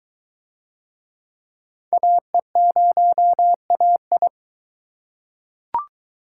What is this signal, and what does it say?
Morse code, heard as a keyed tone, about 23 wpm, 700 Hz. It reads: AE0AI